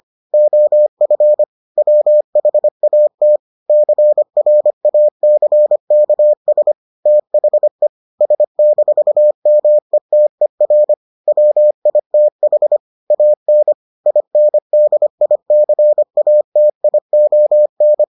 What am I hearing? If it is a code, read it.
Morse code, heard as a keyed tone, about 25 wpm, 600 Hz. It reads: OF WHAT CRACKS THE S-METER WITH AN INDICATION